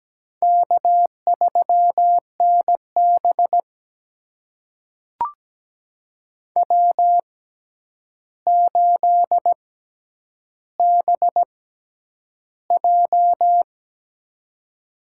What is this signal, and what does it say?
Morse code, heard as a keyed tone, about 17 wpm, spaced as for 9 wpm, 700 Hz. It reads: K3NB W8BJ
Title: USA Call Signs - 4 Characters - 6x Character Spacing - 17wpm